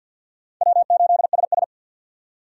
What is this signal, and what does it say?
Morse code, heard as a keyed tone, about 51 wpm, 700 Hz. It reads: W8SS
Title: USA Call Signs - 4 Characters - 3x Character Spacing - 50wpm